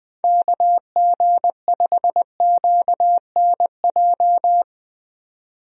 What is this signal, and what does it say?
Morse code, heard as a keyed tone, about 20 wpm, 700 Hz. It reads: KG5QNJ
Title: USA Call Signs - 6 Characters - 20wpm